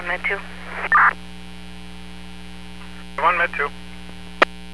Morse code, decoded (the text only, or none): none